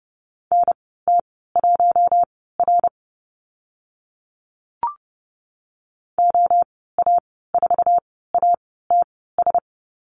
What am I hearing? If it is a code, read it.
Morse code, heard as a keyed tone, about 30 wpm, spaced as for 17 wpm, 700 Hz. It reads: NT1R OA4ATS
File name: Lesson 14g - Number 2 - Call Signs 30wpm